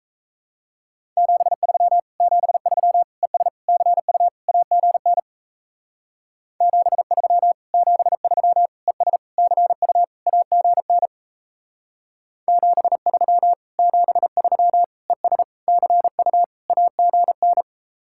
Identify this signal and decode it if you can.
Morse code, heard as a keyed tone, about 34 wpm, 700 Hz. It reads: 73 73 ES CU AGN 73 73 ES CU AGN 73 73 ES CU AGN
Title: Speed-Racing Common QSO Phrases Encore - 28wpm